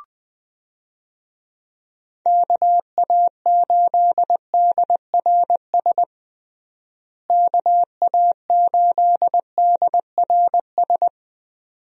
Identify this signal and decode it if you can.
Morse code, heard as a keyed tone, about 20 wpm, 700 Hz. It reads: KA8DRS KA8DRS
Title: USA Call Signs - 6 Characters - Repeated 2x - 20wpm